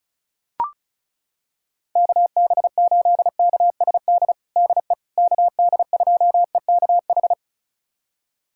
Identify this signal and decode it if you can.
Morse code, heard as a keyed tone, about 36 wpm, 700 Hz. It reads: KB8KSD DE KD2EKH
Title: USA Callsign DE Callsign 35wpm